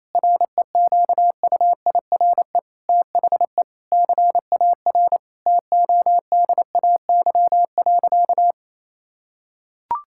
Morse code, heard as a keyed tone, about 28 wpm, 700 Hz. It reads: REQUIRE THE CAR TODAY.
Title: Sentences from Top 1000 Words - Review Part 6 - 28wpm